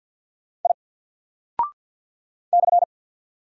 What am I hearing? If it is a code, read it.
Morse code, heard as a keyed tone, about 50 wpm, 700 Hz. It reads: I /
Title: Single Character (Letter:Number:Punctuation:Prosign) 50wpm